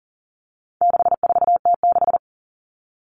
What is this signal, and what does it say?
Morse code, heard as a keyed tone, about 40 wpm, 700 Hz. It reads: 64T6
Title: Common Cut Numbers - 4-Digit 40wpm